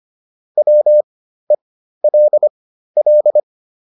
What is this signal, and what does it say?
Morse code, heard as a keyed tone, about 25 wpm, spaced as for 13 wpm, 600 Hz. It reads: WELL